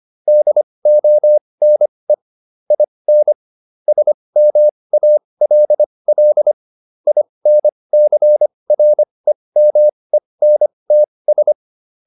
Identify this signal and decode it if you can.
Morse code, heard as a keyed tone, about 25 wpm, spaced as for 20 wpm, 600 Hz. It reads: DONE IN SMALL INCREMENTS
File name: SS316 20